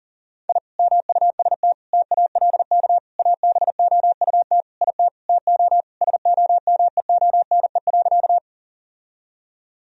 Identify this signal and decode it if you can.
Morse code, heard as a keyed tone, about 40 wpm, 700 Hz. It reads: I MUST TALK ABOUT IT TO SOMEONE.
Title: Sentences from Top 1000 Words - Review Part 4 - 40wpm